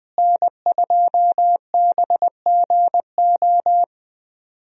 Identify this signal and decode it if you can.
Morse code, heard as a keyed tone, about 20 wpm, 700 Hz. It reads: N2BGO
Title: USA Call Signs - 5 Characters - Repeated 3x - 20wpm